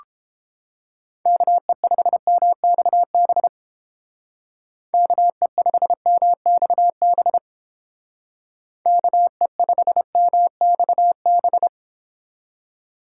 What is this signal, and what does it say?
Morse code, heard as a keyed tone, about 30 wpm, 700 Hz. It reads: KE5MXB KE5MXB KE5MXB